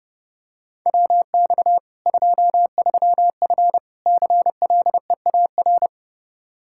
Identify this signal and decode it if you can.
Morse code, heard as a keyed tone, about 30 wpm, 700 Hz. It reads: WX 23F CLEAR
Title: K1USN SST QSO Practice 30wpm